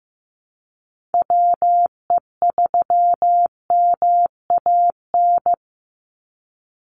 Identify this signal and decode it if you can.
Morse code, heard as a keyed tone, about 15 wpm, 700 Hz. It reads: WE3MAN
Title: USA Call Signs 15wpm